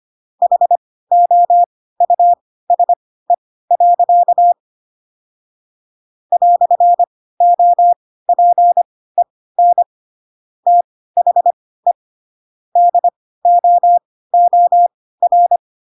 Morse code, heard as a keyed tone, about 25 wpm, spaced as for 16 wpm, 700 Hz. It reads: HOUSE. "OPEN THE DOOR
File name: Pinocchio / Pinocchio ch7 10 25 ik1hge